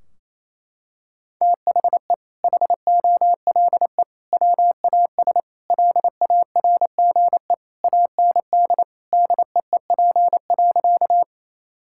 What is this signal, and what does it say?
Morse code, heard as a keyed tone, about 28 wpm, 700 Hz. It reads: THE HOLE WAS LARGE AND DEEP.
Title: Sentences from Top 1600 Words 28wpm